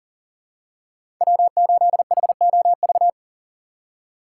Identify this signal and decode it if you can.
Morse code, heard as a keyed tone, about 38 wpm, 700 Hz. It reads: W8HOV
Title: USA Call Signs - 5 Characters - 3x Character Spacing - 40wpm